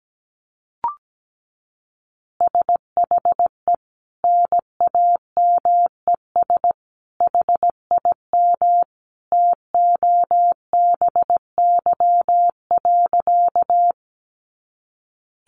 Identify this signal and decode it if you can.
Morse code, heard as a keyed tone, about 17 wpm, 700 Hz. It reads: SHE NAMES HIM TOBY.
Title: Easy Phrases 17wpm